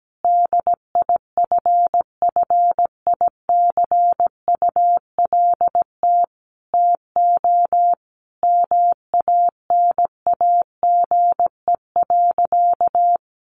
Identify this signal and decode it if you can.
Morse code, heard as a keyed tone, about 17 wpm, 700 Hz. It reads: DIFFICULT TO MANAGE.
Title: Sentences from Top 1400 Words 17wpm